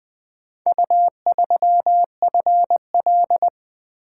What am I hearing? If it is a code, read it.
Morse code, heard as a keyed tone, about 23 wpm, 700 Hz. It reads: U3FL
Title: Speed-Racing - Four Letter-Number Combinations - 20wpm